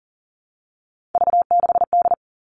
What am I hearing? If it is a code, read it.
Morse code, heard as a keyed tone, about 40 wpm, 700 Hz. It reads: V6D
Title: All Cut Numbers - 3-Digit 40wpm